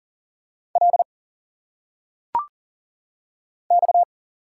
Morse code, heard as a keyed tone, about 39 wpm, 700 Hz. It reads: L X
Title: Single Letters - Phonetic Pronunciation 40wpm